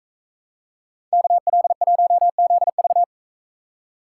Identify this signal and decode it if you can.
Morse code, heard as a keyed tone, about 44 wpm, 700 Hz. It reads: KL1ZV